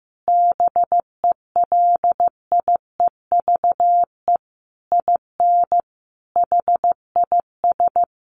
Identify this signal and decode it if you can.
Morse code, heard as a keyed tone, about 14 wpm, 700 Hz. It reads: BELIEVE IN HIS
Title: Sentences from Top 1300 Words 15wpm